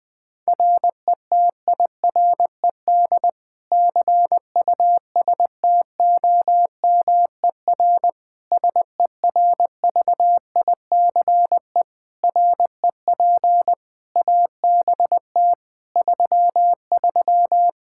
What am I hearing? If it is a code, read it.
Morse code, heard as a keyed tone, about 21 wpm, 700 Hz. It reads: RETIRED CUSTOMER SERVICE REP ABT 33
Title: Practice QSO - Exchange 3 - More Info - 20wpm